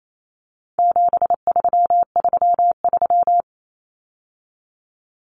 Morse code, heard as a keyed tone, about 28 wpm, 700 Hz. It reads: 7333
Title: Numbers - 4-Digit 28wpm